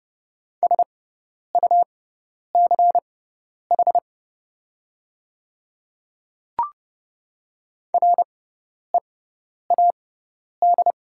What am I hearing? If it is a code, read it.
Morse code, heard as a keyed tone, about 30 wpm, spaced as for 10 wpm, 700 Hz. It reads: SUCH READ
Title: 6x Character Spacing - Top 200 Words - 30wpm